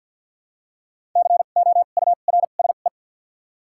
Morse code, heard as a keyed tone, about 48 wpm, spaced as for 35 wpm, 700 Hz. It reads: COURSE
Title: 2x Character Spacing - Top 500 Words - 50wpm